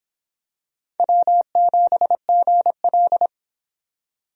Morse code, heard as a keyed tone, about 26 wpm, 700 Hz. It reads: W7GL